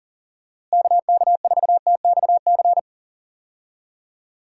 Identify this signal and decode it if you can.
Morse code, heard as a keyed tone, about 40 wpm, 700 Hz. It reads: KK4TXC
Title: USA Call Signs - 6 Characters - Repeated 3x - 40wpm